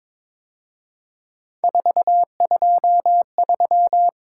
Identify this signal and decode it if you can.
Morse code, heard as a keyed tone, about 26 wpm, 700 Hz. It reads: 423
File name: Speed-Racing - Three Letter-Number Combinations - 22wpm